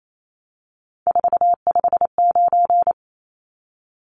Morse code, heard as a keyed tone, about 28 wpm, 700 Hz. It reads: 459
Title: Numbers - 3-Digit 28wpm